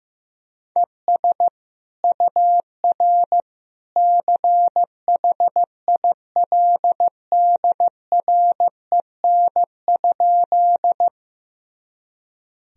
Morse code, heard as a keyed tone, about 15 wpm, 700 Hz. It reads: ES UR CHILDREN?
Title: Sentences from Top 300 Words and Abbreviations 15wpm